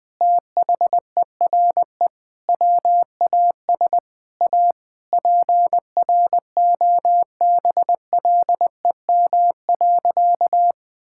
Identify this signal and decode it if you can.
Morse code, heard as a keyed tone, about 21 wpm, 700 Hz. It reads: THERE WAS A PROBLEM.
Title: Sentences with 4-Word Phrases - 20wpm